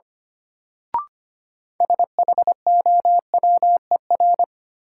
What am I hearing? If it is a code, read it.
Morse code, heard as a keyed tone, about 25 wpm, 700 Hz. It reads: SHOWER
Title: Sentences from Top 1600 Words - New Word Drill - 25wpm